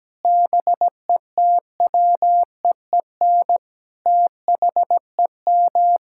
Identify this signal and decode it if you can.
Morse code, heard as a keyed tone, about 17 wpm, 700 Hz. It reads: BETWEEN THEM
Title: Sentences from Top 1000 Words - Review Part 6 - 17wpm